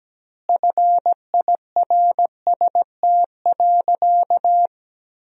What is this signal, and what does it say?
Morse code, heard as a keyed tone, about 17 wpm, 700 Hz. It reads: FIRST.
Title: 2x Word-Spacing - Sentences from Top 300 Words - 17wpm